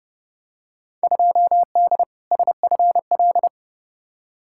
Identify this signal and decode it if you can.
Morse code, heard as a keyed tone, about 32 wpm, 700 Hz. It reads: 2D SFL
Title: Field Day Exchanges Encore - 30wpm